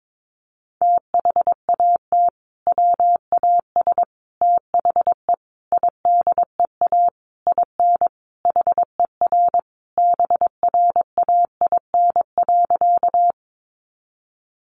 Morse code, heard as a keyed tone, about 22 wpm, 700 Hz. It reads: THAT WAS THE IDEA IN HER BRAIN.
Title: Sentences from Top 1100 Words 22wpm